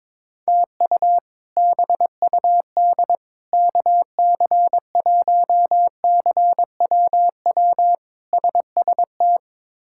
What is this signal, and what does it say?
Morse code, heard as a keyed tone, about 22 wpm, 700 Hz. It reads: TU BUD KC1CWW SST
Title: K1USN SST QSO Practice - 22wpm